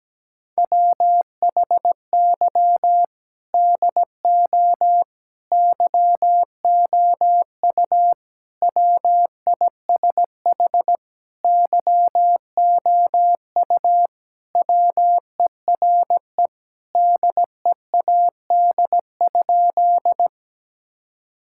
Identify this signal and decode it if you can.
Morse code, heard as a keyed tone, about 17 wpm, 700 Hz. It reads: WHY DO YOU WISH YOU WERE DEAD?
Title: Sentences from Top 1600 Words 17wpm